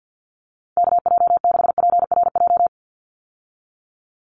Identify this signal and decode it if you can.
Morse code, heard as a keyed tone, about 50 wpm, 700 Hz. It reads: KJ6PRJ
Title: USA Call Signs 50wpm